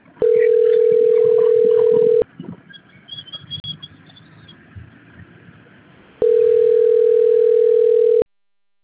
Ringback tone (North America), 440 + 480 Hz.